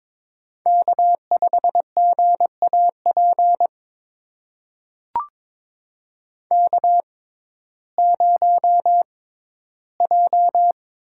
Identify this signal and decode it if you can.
Morse code, heard as a keyed tone, about 22 wpm, spaced as for 13 wpm, 700 Hz. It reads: K5GAP K0J
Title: USA Call Signs - 6x Character Spacing - 22wpm